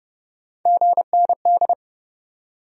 Morse code, heard as a keyed tone, about 40 wpm, 700 Hz. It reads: GND